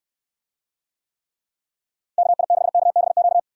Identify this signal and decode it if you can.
Morse code, heard as a keyed tone, about 69 wpm, 700 Hz. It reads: KI6KBQ